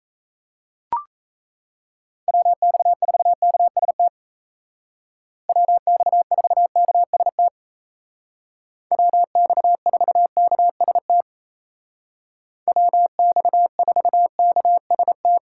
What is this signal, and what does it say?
Morse code, heard as a keyed tone, about 34 wpm, 700 Hz. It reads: WX4KST WX4KST WX4KST WX4KST
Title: Speed Racing - USA Call Signs 28wpm